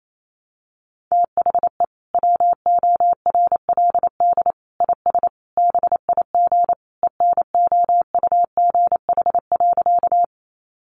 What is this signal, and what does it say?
Morse code, heard as a keyed tone, about 28 wpm, 700 Hz. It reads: THE WORLD IS BIG ENOUGH.
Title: Sentences from Top 400 Words 28wpm